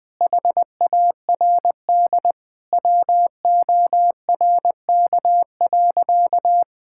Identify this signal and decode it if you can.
Morse code, heard as a keyed tone, about 19 wpm, 700 Hz. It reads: HARD WORK.
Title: Sentences with 4-Word Phrases - 20wpm